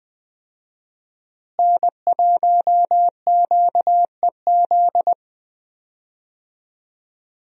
Morse code, heard as a keyed tone, about 20 wpm, 700 Hz. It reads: N1QEZ
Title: USA Call Signs - 5 Characters - Repeated 3x - 20wpm